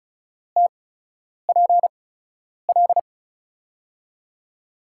Morse code, heard as a keyed tone, about 36 wpm, spaced as for 10 wpm, 700 Hz. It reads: TPL
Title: USA Call Signs - 5 Characters - 8x Character Spacing - 35wpm